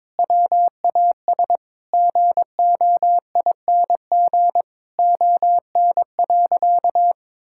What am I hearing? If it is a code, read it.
Morse code, heard as a keyed tone, about 22 wpm, 700 Hz. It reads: WAS GOING ON.
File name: practice / 22wpm / Sentences with 8-Word Phrases - 22wpm